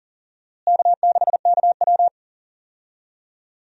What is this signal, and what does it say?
Morse code, heard as a keyed tone, about 40 wpm, 700 Hz. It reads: K6KW
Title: USA Call Signs - 4 Characters - Repeated 2x - 40wpm